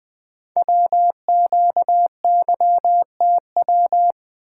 Morse code, heard as a keyed tone, about 20 wpm, 700 Hz. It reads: WQYTW
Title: Code Groups - 5 Letters - 20wpm